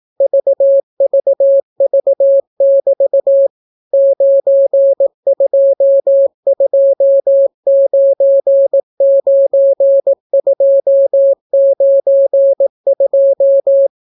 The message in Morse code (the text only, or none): VVV<BT> 92299292